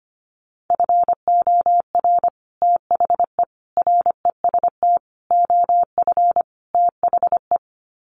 Morse code, heard as a keyed tone, about 25 wpm, 700 Hz. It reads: FOR THE REST OF THE
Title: Sets of 5 Words 25wpm